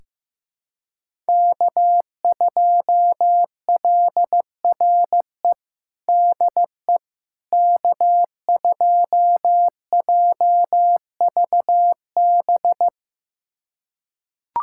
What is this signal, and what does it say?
Morse code, heard as a keyed tone, about 15 wpm, 700 Hz. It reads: K2LRE DE K2JVB
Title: USA Callsign DE Callsign - 5 Characters - 15wpm